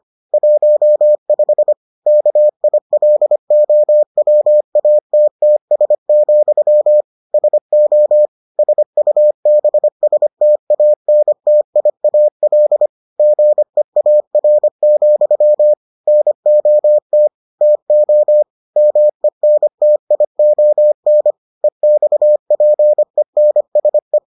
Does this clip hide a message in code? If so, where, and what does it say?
Yes: Morse code from 0.3 to 24.2 s, heard as a keyed tone, about 25 wpm, 600 Hz. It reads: 15 KILOWATTS, SO SUBSTANTIAL GEAR, NOT TO MENTION EXPENSE